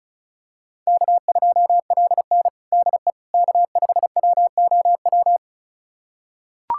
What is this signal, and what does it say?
Morse code, heard as a keyed tone, about 34 wpm, 700 Hz. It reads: K2LN DE K5WOW
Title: USA Callsign DE Callsign 35wpm